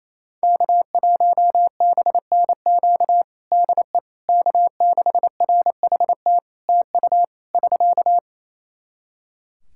Morse code, heard as a keyed tone, about 28 wpm, 700 Hz. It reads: K1BNQ DE K6RHT TU <SK>
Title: K1USN SST QSO Practice 28wpm